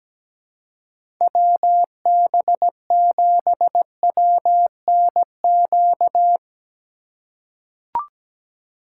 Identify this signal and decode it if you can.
Morse code, heard as a keyed tone, about 17 wpm, 700 Hz. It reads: WB7WNQ
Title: USA Call Signs - 6 Characters - 2x Character Spacing - 17wpm